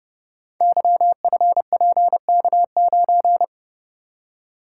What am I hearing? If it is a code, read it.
Morse code, heard as a keyed tone, about 30 wpm, 700 Hz. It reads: YFPK9
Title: Code Groups - 5 Letter:Numbers - 30wpm